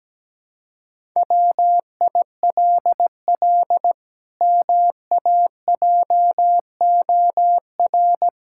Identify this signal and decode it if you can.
Morse code, heard as a keyed tone, about 17 wpm, 700 Hz. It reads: WILL MAJOR
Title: Pangram Challenge v2 - 17wpm